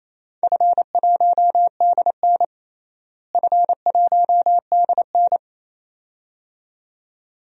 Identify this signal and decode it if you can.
Morse code, heard as a keyed tone, about 28 wpm, 700 Hz. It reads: F1DN F1DN
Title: DXCC-100 Call Signs - 4 Characters - Repeated 2x - 28wpm